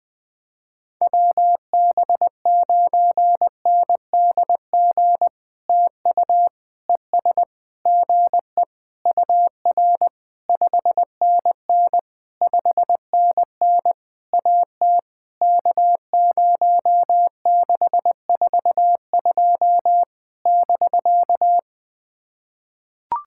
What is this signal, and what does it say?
Morse code, heard as a keyed tone, about 20 wpm, 700 Hz. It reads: WB9NDG TU ES GE UR 5NN 5NN AT K0642 <BK>
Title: POTA QSO Practice - 20wpm